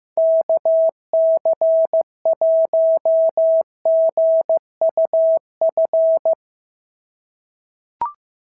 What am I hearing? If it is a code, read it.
Morse code, heard as a keyed tone, about 14 wpm, 650 Hz. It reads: KC1GUF
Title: SOTA QSO Practice 15wpm